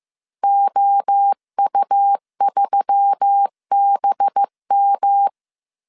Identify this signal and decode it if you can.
Morse code, heard as a keyed tone, about 15 wpm, 800 Hz. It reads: OU3BM
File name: CW / DET / 1 6 1500C S C 14wpm 2951